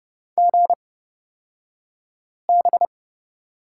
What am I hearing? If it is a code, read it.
Morse code, heard as a keyed tone, about 31 wpm, spaced as for 5 wpm, 700 Hz. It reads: GB
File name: Lesson 38d - Letter q - Two Characters 30wpm